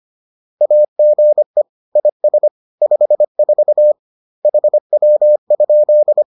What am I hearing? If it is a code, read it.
Morse code, heard as a keyed tone, about 25 wpm, 600 Hz. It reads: AGE IS 54 HW?